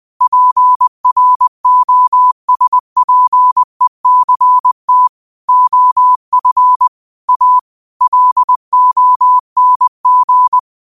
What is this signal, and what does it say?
Morse code, heard as a keyed tone, about 20 wpm, 1000 Hz. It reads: PROSPECT OF A LONG
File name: qotdfiles / 20wpm